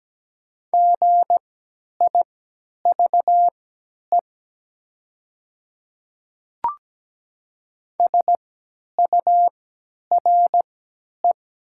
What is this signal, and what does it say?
Morse code, heard as a keyed tone, about 17 wpm, spaced as for 10 wpm, 700 Hz. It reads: GIVE SURE
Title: 3x Character Spacing - Top 300 Words - 17wpm